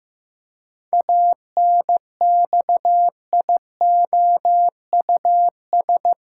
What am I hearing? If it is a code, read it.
Morse code, heard as a keyed tone, about 15 wpm, 700 Hz. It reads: ANXIOUS
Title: Pangram Challenge v2 - 15wpm